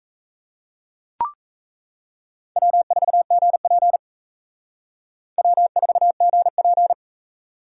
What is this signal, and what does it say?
Morse code, heard as a keyed tone, about 35 wpm, 700 Hz. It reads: W4GP W4GP